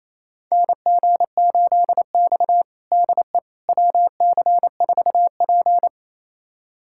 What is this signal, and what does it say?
Morse code, heard as a keyed tone, about 28 wpm, 700 Hz. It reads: NG8X DE WC4P
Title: USA Callsign DE Callsign - 4 Characters - 28wpm